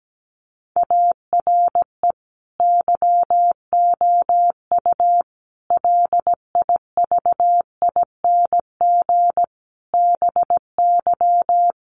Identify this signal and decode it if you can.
Morse code, heard as a keyed tone, about 17 wpm, 700 Hz. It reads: ARE YOU LIVING BY
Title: Sentences from Top 1100 Words 17wpm